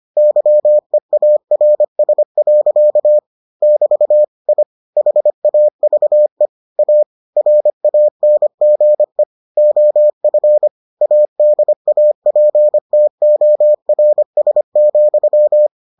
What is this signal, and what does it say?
Morse code, heard as a keyed tone, about 25 wpm, 600 Hz. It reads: YEARS. <BT> I HAVE A RANGE OF ADAPTORS,